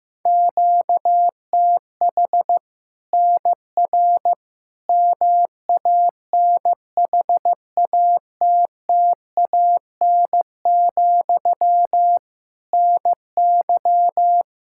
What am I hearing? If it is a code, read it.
Morse code, heard as a keyed tone, about 15 wpm, 700 Hz. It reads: QTH NR MANHATTAN, NY
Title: Practice QSO - Exchange 1 - Beginning - 15wpm